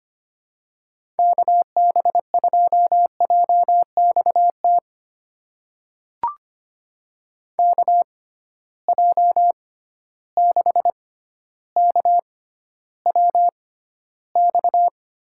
Morse code, heard as a keyed tone, about 25 wpm, spaced as for 13 wpm, 700 Hz. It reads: KB2JXT KJ6KWX
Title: USA Call Signs - 6 Characters - 6x Character Spacing - 25wpm